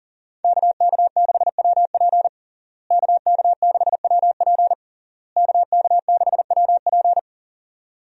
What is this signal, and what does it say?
Morse code, heard as a keyed tone, about 40 wpm, 700 Hz. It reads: KK6WP KK6WP KK6WP
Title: USA Call Signs - 5 Characters - Repeated 3x - 40wpm